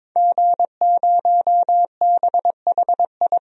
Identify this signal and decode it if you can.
Morse code, heard as a keyed tone, about 21 wpm, 700 Hz. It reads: G0BHI
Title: DXCC-100 Call Signs - 5 Characters - 22wpm